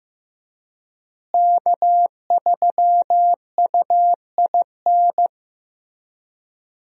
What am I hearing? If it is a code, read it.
Morse code, heard as a keyed tone, about 15 wpm, 700 Hz. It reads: K3UIN
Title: USA Call Signs - 5 Characters - Repeated 3x - 15wpm